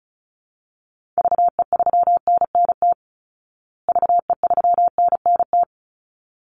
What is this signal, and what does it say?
Morse code, heard as a keyed tone, about 34 wpm, 700 Hz. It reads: VE3NNT VE3NNT
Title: Call Signs Repeated 3x - 35wpm